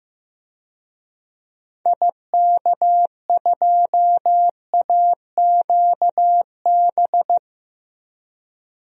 Morse code, heard as a keyed tone, about 15 wpm, 700 Hz. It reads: IK2AQB
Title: DXCC-100 Call Signs - 6 Characters - 15wpm